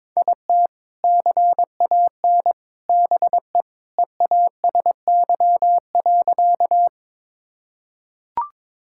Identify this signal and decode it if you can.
Morse code, heard as a keyed tone, about 23 wpm, 700 Hz. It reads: IT CAN BE EASY.
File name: practice / 22wpm / Sentences with 3-Word Phrases - 22wpm